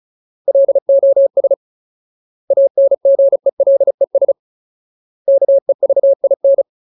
Morse code, heard as a keyed tone, about 34 wpm, 550 Hz. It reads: LOS ANGELES KEVIN
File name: Ultimate Word List / 35WPM Ultimate